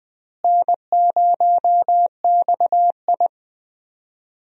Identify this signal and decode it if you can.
Morse code, heard as a keyed tone, about 20 wpm, 700 Hz. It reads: N0XI
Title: USA Call Signs - 4 Characters - Repeated 3x - 20wpm